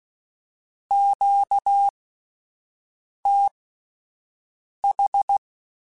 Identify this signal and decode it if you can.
Morse code, heard as a keyed tone, about 16 wpm, spaced as for 5 wpm, 800 Hz. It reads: QTH
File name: (spaced) / QSOs Spaced-mp3s / kc7haa @10 sp